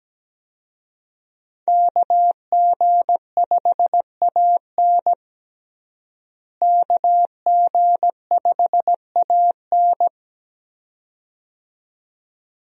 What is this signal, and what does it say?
Morse code, heard as a keyed tone, about 17 wpm, 700 Hz. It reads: KG5AN KG5AN
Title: USA Call Signs - 5 Characters - Repeated 2x - 17wpm